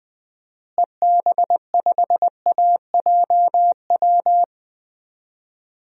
Morse code, heard as a keyed tone, about 21 wpm, 700 Hz. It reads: EB5AJW
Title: DXCC-100 Call Signs - 6 Characters - Repeated 3x - 20wpm